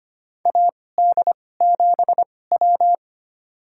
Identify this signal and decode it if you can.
Morse code, heard as a keyed tone, about 25 wpm, spaced as for 18 wpm, 700 Hz. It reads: AD7W
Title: USA Call Signs - 4 Characters - 2x Character Spacing - 25wpm